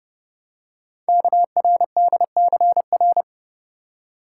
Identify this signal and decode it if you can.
Morse code, heard as a keyed tone, about 30 wpm, 700 Hz. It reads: KRDCR
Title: Code Groups - 5 Letters - 30wpm